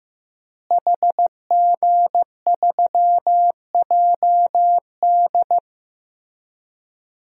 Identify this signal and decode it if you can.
Morse code, heard as a keyed tone, about 15 wpm, 700 Hz. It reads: HG3JD
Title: DXCC-100 Call Signs - 5 Characters - Repeated 3x - 15wpm